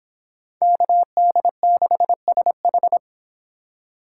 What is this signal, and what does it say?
Morse code, heard as a keyed tone, about 29 wpm, 700 Hz. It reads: KD6SH